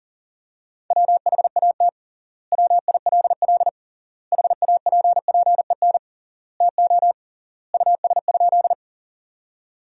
Morse code, heard as a keyed tone, about 37 wpm, 700 Hz. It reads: WHAT WILL HAPPEN TO US?
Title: 3x Word-Spacing - Sentences from Top 400 Words - 40wpm